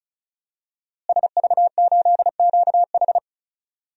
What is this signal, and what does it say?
Morse code, heard as a keyed tone, about 34 wpm, 700 Hz. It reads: SV8QH
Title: DXCC-100 Call Signs - 5 Characters - Repeated 2x - 35wpm